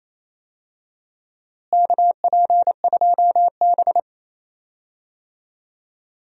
Morse code, heard as a keyed tone, about 28 wpm, 700 Hz. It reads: KP2B
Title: DXCC-100 Call Signs - 4 Characters - Repeated 3x - 28wpm